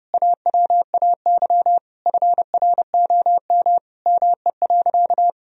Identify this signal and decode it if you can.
Morse code, heard as a keyed tone, about 30 wpm, 700 Hz. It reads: AWAY FROM ME.
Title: Sentences from Top 1000 Words - Review Part 2 - 30wpm